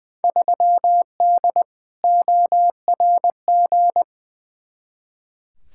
Morse code, heard as a keyed tone, about 20 wpm, 700 Hz. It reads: 3D ORG
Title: Field Day Exchanges Encore - 20wpm